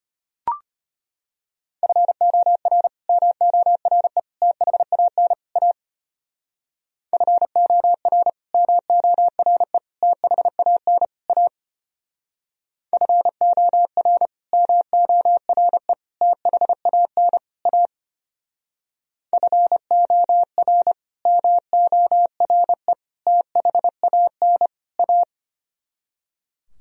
Morse code, heard as a keyed tone, about 32 wpm, 700 Hz. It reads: FOR MORE THAN A FOR MORE THAN A FOR MORE THAN A FOR MORE THAN A